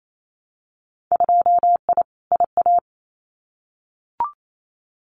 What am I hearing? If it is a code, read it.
Morse code, heard as a keyed tone, about 28 wpm, 700 Hz. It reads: 2I IA
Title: Winter Field Day Exchanges Encore 28wpm